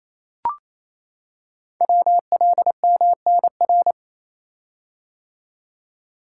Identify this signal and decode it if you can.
Morse code, heard as a keyed tone, about 27 wpm, 700 Hz. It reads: WLMNR
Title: Code Groups - 5 Letters - 28wpm